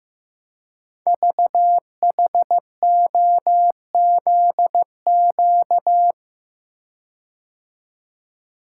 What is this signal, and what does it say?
Morse code, heard as a keyed tone, about 15 wpm, 700 Hz. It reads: VHOZQ
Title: Code Groups - 5 Letters - 15wpm